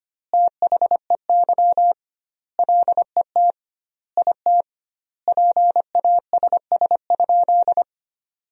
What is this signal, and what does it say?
Morse code, heard as a keyed tone, about 25 wpm, 700 Hz. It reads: THEY LET IT PASS?
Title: 2x Word-Spacing - Sentences from Top 400 Words - 25wpm